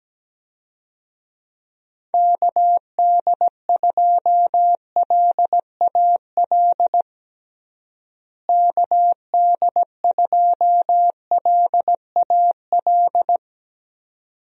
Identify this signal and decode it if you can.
Morse code, heard as a keyed tone, about 17 wpm, 700 Hz. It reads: KD2LAL KD2LAL
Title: USA Call Signs - 6 Characters - Repeated 2x - 17wpm